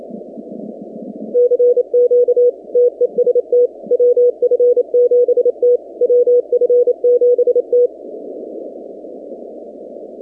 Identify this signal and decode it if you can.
Morse code, heard as a keyed tone, about 28 wpm, 490 Hz. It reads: CQ TEST WF7T WF7T
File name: 2019-cw-wpx-kt5j-k5pi